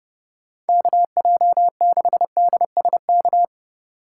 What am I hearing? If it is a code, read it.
Morse code, heard as a keyed tone, about 30 wpm, 700 Hz. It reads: KJ6DSK
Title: USA Call Signs - 6 Characters - 4x Character Spacing - 30wpm